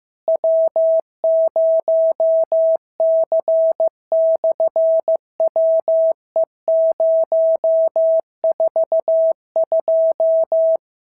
Morse code, heard as a keyed tone, about 14 wpm, 650 Hz. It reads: W0C/WE042
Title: SOTA QSO Practice 15wpm